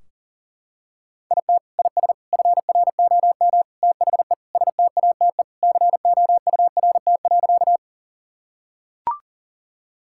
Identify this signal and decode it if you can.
Morse code, heard as a keyed tone, about 40 wpm, 700 Hz. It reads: IT IS FROM THE STATE COURT.
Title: Sentences from Top 1600 Words 40wpm